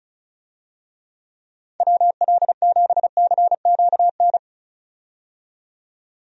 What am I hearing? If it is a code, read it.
Morse code, heard as a keyed tone, about 36 wpm, 700 Hz. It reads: WL7CQN
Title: USA Call Signs - 6 Characters - 35wpm